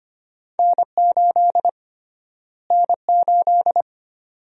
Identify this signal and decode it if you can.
Morse code, heard as a keyed tone, about 25 wpm, 700 Hz. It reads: N8 N8
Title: USA Call Sign Prefixes - Repeated 2x - 25wpm